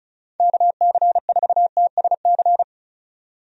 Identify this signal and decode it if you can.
Morse code, heard as a keyed tone, about 36 wpm, 700 Hz. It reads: KC4TSC
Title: USA Call Signs - 6 Characters - 35wpm